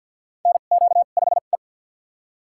Morse code, heard as a keyed tone, about 47 wpm, spaced as for 35 wpm, 700 Hz. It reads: NQ5E
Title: USA Call Signs - 4 Characters - 2x Character Spacing - 50wpm